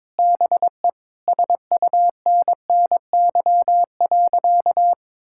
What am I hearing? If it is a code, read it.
Morse code, heard as a keyed tone, about 22 wpm, 700 Hz. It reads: BE SUNNY.
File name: practice / 22wpm / Sentences with 4-Word Phrases - 22wpm